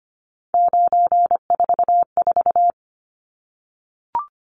Morse code, heard as a keyed tone, about 25 wpm, 700 Hz. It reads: 944
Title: Numbers - 3-Digit 25wpm